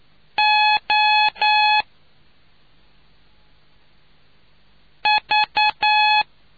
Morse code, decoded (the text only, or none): OV